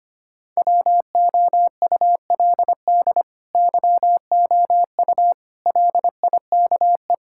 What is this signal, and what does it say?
Morse code, heard as a keyed tone, about 25 wpm, 700 Hz. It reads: WOULD YOU LIKE
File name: practice / 25wpm / Sentences with 2-Word Phrases - 25wpm